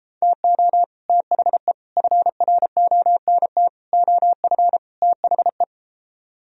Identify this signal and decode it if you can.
Morse code, heard as a keyed tone, about 33 wpm, 700 Hz. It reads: TO THE FRONT OF THE